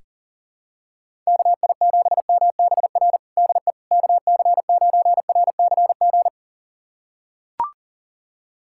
Morse code, heard as a keyed tone, about 37 wpm, 700 Hz. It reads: KI7MBR DE KC9RCG
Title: USA Callsign DE Callsign - 6 Characters - 40wpm